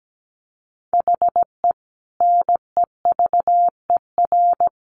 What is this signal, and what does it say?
Morse code, heard as a keyed tone, about 17 wpm, 700 Hz. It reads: HE NEVER
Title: Sentences from Top 1100 Words 17wpm